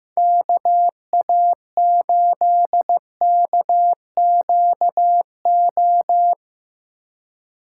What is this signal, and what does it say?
Morse code, heard as a keyed tone, about 15 wpm, 700 Hz. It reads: KA8KQO
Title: USA Call Signs - 3x Character Spacing - 15wpm